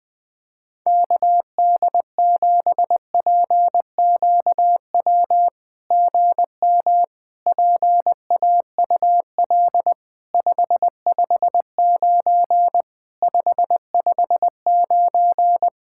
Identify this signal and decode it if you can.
Morse code, heard as a keyed tone, about 20 wpm, 700 Hz. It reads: KD7PQW GM PAUL 559 559
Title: SOTA QSO Practice 20wpm